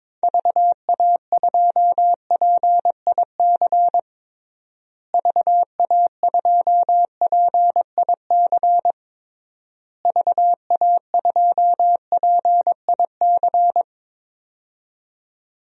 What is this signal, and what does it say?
Morse code, heard as a keyed tone, about 22 wpm, 700 Hz. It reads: VA2PIC VA2PIC VA2PIC
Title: DXCC-100 Call Signs - 6 Characters - Repeated 3x - 22wpm